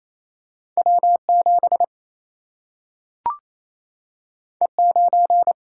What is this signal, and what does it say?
Morse code, heard as a keyed tone, about 28 wpm, 700 Hz. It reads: W 7 E 9
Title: Single-Letter Single-Number 28wpm